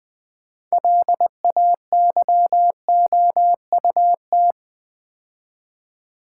Morse code, heard as a keyed tone, about 20 wpm, 690 Hz. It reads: LAYOUT
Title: Build A Word - 2 Letters At A Time - 6-Letter Words - 20wpm